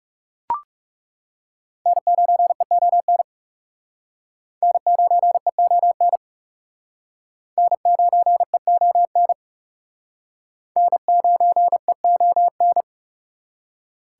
Morse code, heard as a keyed tone, about 35 wpm, 700 Hz. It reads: N9EON N9EON N9EON N9EON
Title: Speed Racing - USA 1x3 Call Signs 30wpm